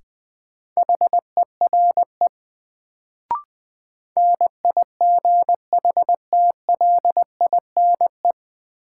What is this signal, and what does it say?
Morse code, heard as a keyed tone, about 20 wpm, 700 Hz. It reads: HERE NIGHTLINE
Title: Sets of 5 Words - New Word Drill - 20wpm